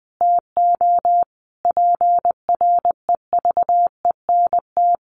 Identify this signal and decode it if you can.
Morse code, heard as a keyed tone, about 20 wpm, 700 Hz. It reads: TO PREVENT
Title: Sentences from Top 1100 Words 20wpm